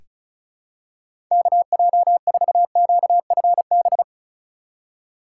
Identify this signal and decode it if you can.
Morse code, heard as a keyed tone, about 34 wpm, 700 Hz. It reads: KJ4QFB
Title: USA Call Signs - 6 Characters - 3x Character Spacing - 35wpm